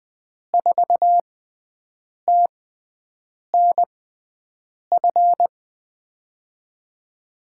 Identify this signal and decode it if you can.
Morse code, heard as a keyed tone, about 21 wpm, spaced as for 7 wpm, 700 Hz. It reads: 4TNF
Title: USA Call Signs - 5 Characters - 6x Character Spacing - 20wpm